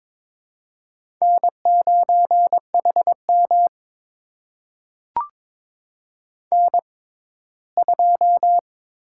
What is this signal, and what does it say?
Morse code, heard as a keyed tone, about 22 wpm, spaced as for 15 wpm, 700 Hz. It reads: N9HM N2
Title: USA Call Signs - 4 Characters - 6x Character Spacing - 22wpm